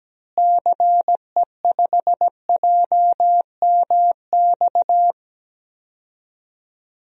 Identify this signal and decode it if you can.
Morse code, heard as a keyed tone, about 17 wpm, 700 Hz. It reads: CE5JMX
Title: DXCC-100 Call Signs - 6 Characters - Repeated 2x - 17wpm